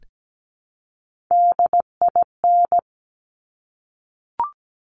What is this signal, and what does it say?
Morse code, heard as a keyed tone, about 17 wpm, 700 Hz. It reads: DIN
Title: Top Three Letter Combinations in Words 17wpm